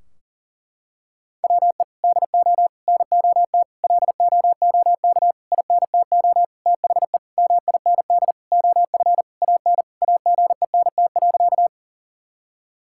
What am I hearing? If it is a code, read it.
Morse code, heard as a keyed tone, about 40 wpm, 700 Hz. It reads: WE DO NOT LOOK INTO THE MIND OF AN AGENT.
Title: Sentences from Top 1600 Words 40wpm